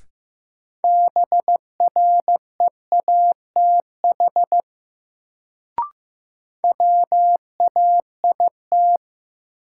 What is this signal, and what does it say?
Morse code, heard as a keyed tone, about 16 wpm, 700 Hz. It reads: BREATH WAIT
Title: Sets of 5 Words Encore - New Word Drill - 15wpm